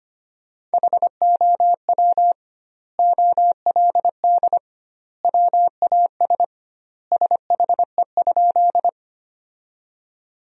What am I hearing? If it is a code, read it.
Morse code, heard as a keyed tone, about 25 wpm, 700 Hz. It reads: HOW OLD WAS SHE?
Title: 2x Word-Spacing - Sentences from Top 200 Words - 25wpm